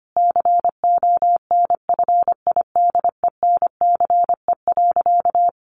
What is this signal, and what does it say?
Morse code, heard as a keyed tone, about 25 wpm, 700 Hz. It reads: CONFIDENCE.
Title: Sentences from Top 1500 Words 25wpm